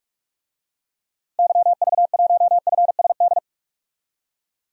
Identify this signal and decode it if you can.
Morse code, heard as a keyed tone, about 46 wpm, 700 Hz. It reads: YV1FSD
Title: DXCC-100 Call Signs - 6 Characters - Repeated 2x - 45wpm